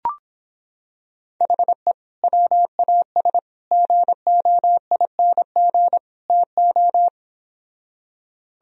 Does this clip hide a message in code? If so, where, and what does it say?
Yes: Morse code from 1.4 to 7.1 s, heard as a keyed tone, about 26 wpm, 700 Hz. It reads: HE WAS GOING TO